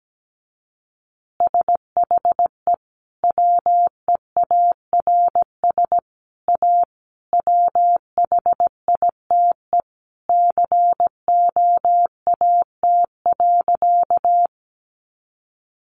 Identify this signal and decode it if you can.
Morse code, heard as a keyed tone, about 17 wpm, 700 Hz. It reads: SHE WEARS A WHITE COAT.
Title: Additional Easy Phrases 17wpm